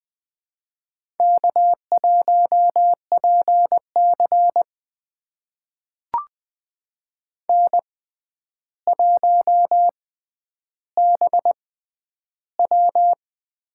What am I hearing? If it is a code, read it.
Morse code, heard as a keyed tone, about 20 wpm, spaced as for 10 wpm, 700 Hz. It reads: K1PC N1BW
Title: USA Call Signs - 4 Characters - 6x Character Spacing - 20wpm